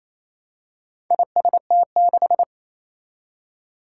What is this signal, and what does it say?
Morse code, heard as a keyed tone, about 28 wpm, 700 Hz. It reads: IST6
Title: Four Challenging Letter-Number Combinations 28wpm